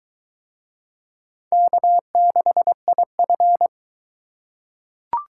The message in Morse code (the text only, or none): K6IF